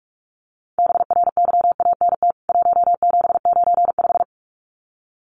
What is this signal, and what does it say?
Morse code, heard as a keyed tone, about 44 wpm, 700 Hz. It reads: BRYANT 1795
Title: CWT Exchange 45wpm